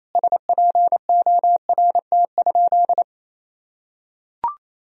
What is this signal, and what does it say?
Morse code, heard as a keyed tone, about 28 wpm, 700 Hz. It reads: SPORT?
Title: Sentences from Top 1000 Words - Review Part 4 - 28wpm